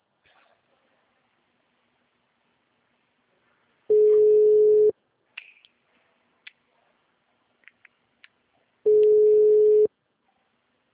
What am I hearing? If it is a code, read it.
Ringback tone (Europe, 425 Hz).